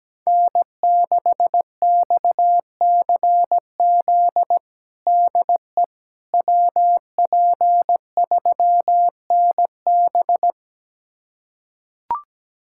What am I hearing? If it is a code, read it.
Morse code, heard as a keyed tone, about 17 wpm, 700 Hz. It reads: N6XCZ DE WP3NB
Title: USA Callsign DE Callsign - 5 Characters - 17wpm